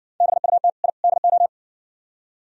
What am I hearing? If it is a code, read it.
Morse code, heard as a keyed tone, about 59 wpm, 700 Hz. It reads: BUT I DO